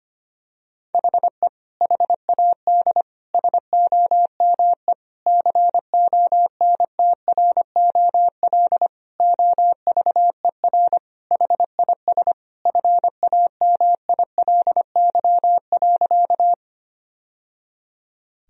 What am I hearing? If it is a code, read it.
Morse code, heard as a keyed tone, about 25 wpm, 700 Hz. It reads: HE HAD SOME CONTROL OVER HIS FAMILY.
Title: Sentences from Top 1000 Words - Review Part 3 - 25wpm